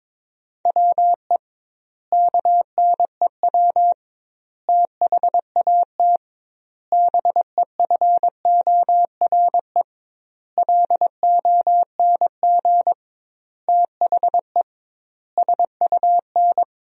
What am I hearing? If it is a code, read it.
Morse code, heard as a keyed tone, about 22 wpm, 700 Hz. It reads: WE KNEW THAT BEFORE LONG THE SUN
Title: 2x Word-Spacing - Sentences from Top 400 Words - 22wpm